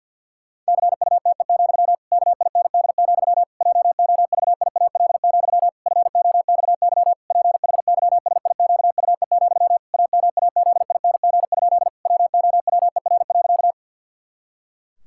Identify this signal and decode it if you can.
Morse code, heard as a keyed tone, about 51 wpm, 700 Hz. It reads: CUTE, KIND, JOVIAL, FOXY PHYSIQUE, AMAZING? WOWER.
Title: Pangram Challenge v2 - 50wpm